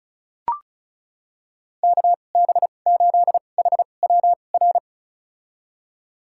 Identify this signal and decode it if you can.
Morse code, heard as a keyed tone, about 36 wpm, spaced as for 25 wpm, 700 Hz. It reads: KB8HWR
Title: USA Call Signs - 6 Characters - 2x Character Spacing - 35wpm